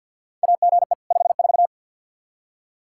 Morse code, heard as a keyed tone, about 52 wpm, 700 Hz. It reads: AGE 54
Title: Practice QSO - Exchange 3 - More Info - 50wpm